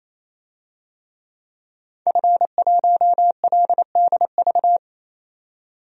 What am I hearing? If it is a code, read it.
Morse code, heard as a keyed tone, about 28 wpm, 700 Hz. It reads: F1LDV
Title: DXCC-100 Call Signs - 5 Characters - 28wpm